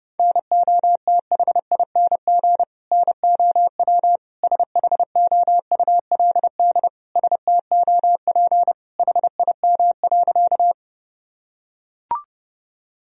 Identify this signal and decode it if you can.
Morse code, heard as a keyed tone, about 30 wpm, 700 Hz. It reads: NOTHING NOW SHOULD STOP HIM.
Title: Sentences from Top 1000 Words - Review Part 1 - 30wpm